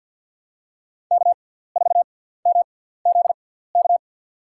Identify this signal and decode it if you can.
Morse code, heard as a keyed tone, about 52 wpm, spaced as for 17 wpm, 690 Hz. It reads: K4MZK
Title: USA Call Signs - 5 Characters - 6x Character Spacing - 50wpm